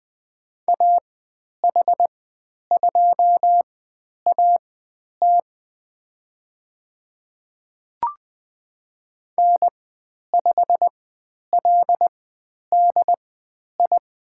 Morse code, heard as a keyed tone, about 20 wpm, spaced as for 10 wpm, 700 Hz. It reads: AH2AT N5LDI
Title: 102 Lesson 17g - Letter u - Call Signs 20wpm